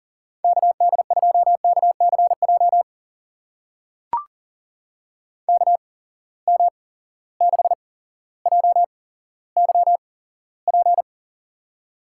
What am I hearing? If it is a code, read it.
Morse code, heard as a keyed tone, about 40 wpm, spaced as for 17 wpm, 700 Hz. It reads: KD2KCJ KM6JYP
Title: USA Call Signs - 6 Characters - 8x Character Spacing - 40wpm